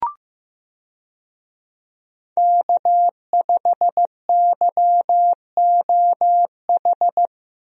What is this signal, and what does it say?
Morse code, heard as a keyed tone, about 15 wpm, 700 Hz. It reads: K5YOH
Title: USA Call Signs - 5 Characters - Repeated 2x - 15wpm